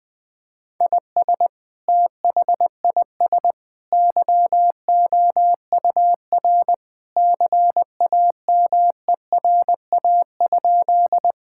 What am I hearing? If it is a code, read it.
Morse code, heard as a keyed tone, about 20 wpm, 700 Hz. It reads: IS THIS YOUR CAMERA?
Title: Sentences from Top 1000 Words - Review Part 5 - 20wpm